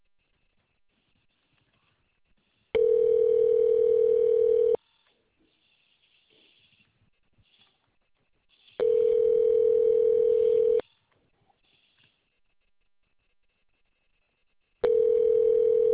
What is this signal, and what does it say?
Ringback tone (North America), 440 + 480 Hz.